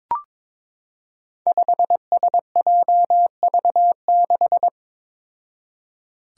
Morse code, heard as a keyed tone, about 21 wpm, 700 Hz. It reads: 5SJV6
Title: Five Challenging Letter-Number Combinations 22wpm